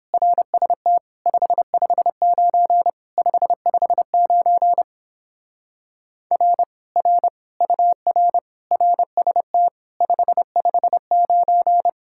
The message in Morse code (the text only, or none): RST 559 559 R R UR RST 559